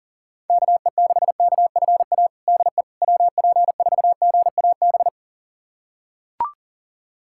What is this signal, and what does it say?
Morse code, heard as a keyed tone, about 40 wpm, 700 Hz. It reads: KE6KFA DE WP4GAB
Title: USA Callsign DE Callsign - 6 Characters - 40wpm